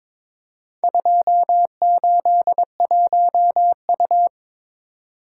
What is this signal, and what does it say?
Morse code, heard as a keyed tone, about 22 wpm, 700 Hz. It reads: 281U
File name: practice / 22wpm / Four Letter-Number Combinations - 22wpm